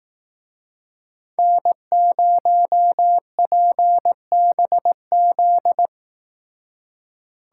Morse code, heard as a keyed tone, about 18 wpm, 700 Hz. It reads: N0PBZ
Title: Speed Racing - USA 1x3 Call Signs 15wpm